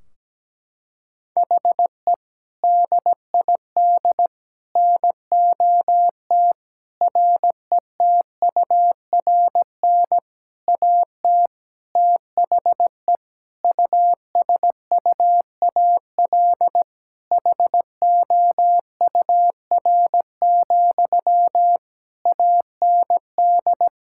Morse code, heard as a keyed tone, about 17 wpm, 700 Hz. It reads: HE DID NOT RETURN AT THE USUAL HOUR, AND
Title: Sentences from Top 1600 Words 17wpm